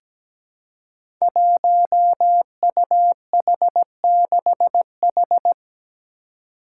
Morse code, heard as a keyed tone, about 17 wpm, 700 Hz. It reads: 1UH6H
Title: Five Challenging Letter-Number Combinations 17wpm